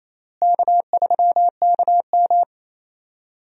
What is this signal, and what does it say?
Morse code, heard as a keyed tone, about 27 wpm, 700 Hz. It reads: K3KM
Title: USA Call Signs - 4 Characters - Repeated 3x - 28wpm